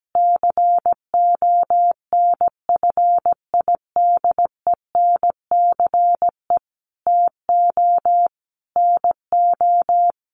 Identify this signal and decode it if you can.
Morse code, heard as a keyed tone, about 17 wpm, 700 Hz. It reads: CONFIDENCE TO NO
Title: Sentences from Top 1500 Words 17wpm